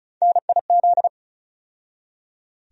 Morse code, heard as a keyed tone, about 36 wpm, 700 Hz. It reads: NIZ
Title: USA Call Sign Postfixes - 3 Letters 35wpm